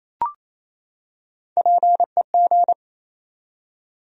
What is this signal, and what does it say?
Morse code, heard as a keyed tone, about 28 wpm, 700 Hz. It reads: PEG
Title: USA Common Names - Max 3 Letters - 28wpm